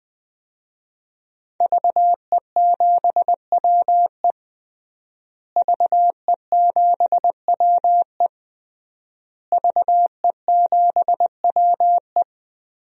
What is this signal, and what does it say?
Morse code, heard as a keyed tone, about 20 wpm, 700 Hz. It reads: VE7WE VE7WE VE7WE
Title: DXCC-100 Call Signs - 5 Characters - Repeated 3x - 20wpm